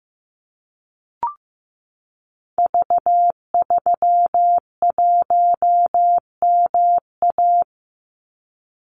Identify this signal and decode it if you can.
Morse code, heard as a keyed tone, about 15 wpm, 700 Hz. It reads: V31MA
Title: Call Signs Repeated 3x - 15wpm